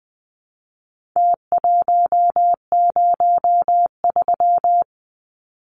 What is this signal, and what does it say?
Morse code, heard as a keyed tone, about 20 wpm, 700 Hz. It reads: T103
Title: Numbers - Contest Style - 4-Digit 20wpm